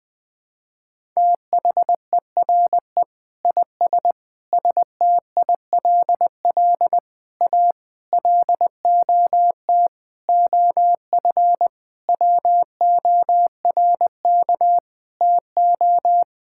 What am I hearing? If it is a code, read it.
Morse code, heard as a keyed tone, about 20 wpm, 700 Hz. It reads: THERE IS STILL A LOT OF WORK TO
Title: Sets of 9 Words - 20wpm